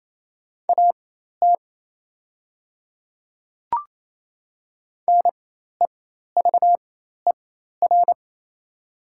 Morse code, heard as a keyed tone, about 28 wpm, spaced as for 13 wpm, 700 Hz. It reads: AT NEVER
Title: 4x Character Spacing - Top 300 Words - 28wpm